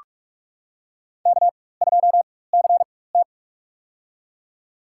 Morse code, heard as a keyed tone, about 46 wpm, spaced as for 21 wpm, 700 Hz. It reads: K2CT
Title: USA Call Signs - 4x Character Spacing - 45wpm